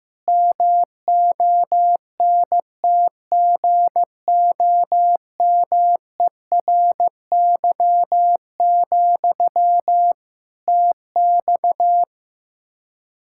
Morse code, heard as a keyed tone, about 15 wpm, 700 Hz. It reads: MONTGOMERY, TX
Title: Practice QSO - Exchange 1 - Beginning - 15wpm